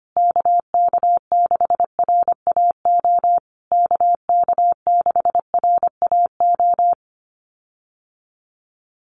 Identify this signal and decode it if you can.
Morse code, heard as a keyed tone, about 25 wpm, 700 Hz. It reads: KK6RAO KK6RAO
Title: USA Call Signs Repeated 3x 25wpm